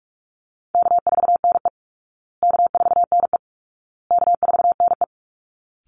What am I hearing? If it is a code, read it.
Morse code, heard as a keyed tone, about 44 wpm, 700 Hz. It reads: K4NE K4NE K4NE
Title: Easy Call Signs Repeated 3x - 45wpm